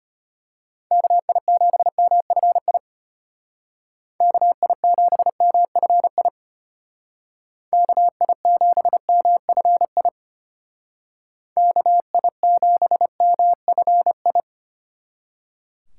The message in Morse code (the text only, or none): KI7MFI KI7MFI KI7MFI KI7MFI